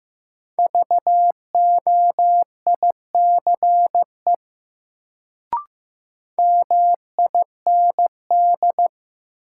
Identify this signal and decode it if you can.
Morse code, heard as a keyed tone, about 15 wpm, 700 Hz. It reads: VOICE MIND
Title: Sentences from Top 500 Words - New Word Drill - 15wpm